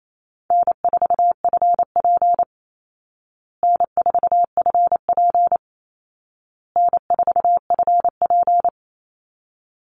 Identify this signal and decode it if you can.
Morse code, heard as a keyed tone, about 28 wpm, 700 Hz. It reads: N4FP N4FP N4FP
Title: Call Signs Repeated 2x - 28wpm